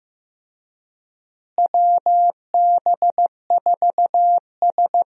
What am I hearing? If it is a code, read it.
Morse code, heard as a keyed tone, about 15 wpm, 700 Hz. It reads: WB4S
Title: USA Call Signs - 4 Characters - Repeated 2x - 15wpm